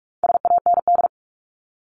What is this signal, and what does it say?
Morse code, heard as a keyed tone, about 44 wpm, 700 Hz. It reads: SAND
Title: Top 1501-1750 Words 45wpm